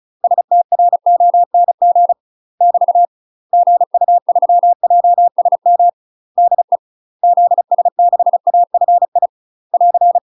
Morse code, heard as a keyed tone, about 34 wpm, 700 Hz. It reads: STRONG <BT> GU3JSM DE ZS6AFI <AR>
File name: UK MASTER QSOs TRAINING / 35WPM / UK MASTER QSOs 0019